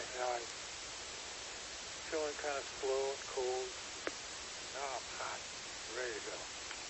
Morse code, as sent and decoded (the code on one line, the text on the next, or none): none